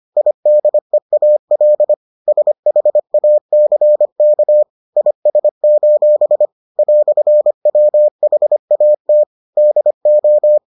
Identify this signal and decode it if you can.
Morse code, heard as a keyed tone, about 25 wpm, 600 Hz. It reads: IDEAL SHACK IS: "WHAT DO